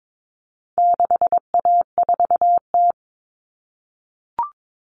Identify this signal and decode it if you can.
Morse code, heard as a keyed tone, about 22 wpm, 700 Hz. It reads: 6A4T
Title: Common Cut Numbers - 4-Digit 22wpm